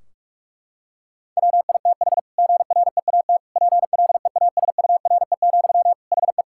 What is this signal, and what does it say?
Morse code, heard as a keyed tone, about 41 wpm, 700 Hz. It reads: WITH GREAT PLEASURE, HE
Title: Sentences from Top 1600 Words 45wpm